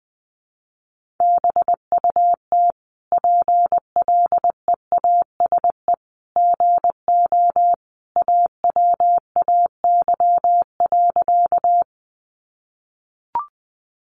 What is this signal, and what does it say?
Morse code, heard as a keyed tone, about 20 wpm, 700 Hz. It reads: BUT PLEASE GO AWAY.
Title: Sentences from Top 1500 Words 20wpm